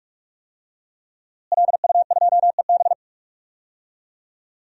Morse code, heard as a keyed tone, about 44 wpm, 700 Hz. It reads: LU2EB
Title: DXCC-100 Call Signs - 5 Characters - Repeated 3x - 45wpm